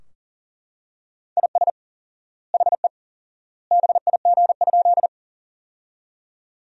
Morse code, heard as a keyed tone, about 40 wpm, 700 Hz. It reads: IS HE BIG?
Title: 4x Word-Spacing - Sentences from Top 200 Words - 40wpm